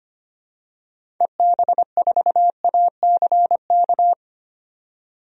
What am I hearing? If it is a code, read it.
Morse code, heard as a keyed tone, about 25 wpm, 700 Hz. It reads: EB4ACK
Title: DXCC-100 Call Signs - 6 Characters - 25wpm